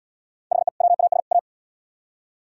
Morse code, heard as a keyed tone, about 77 wpm, 700 Hz. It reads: HE WAS A